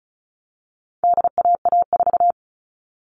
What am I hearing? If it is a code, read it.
Morse code, heard as a keyed tone, about 34 wpm, 700 Hz. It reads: DAA4
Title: All Cut Numbers - 4-Digit 35wpm